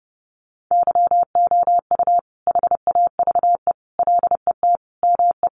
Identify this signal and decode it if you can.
Morse code, heard as a keyed tone, about 30 wpm, 700 Hz. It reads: YOU HAVE LET ME
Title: Sentences from Top 1500 Words 30wpm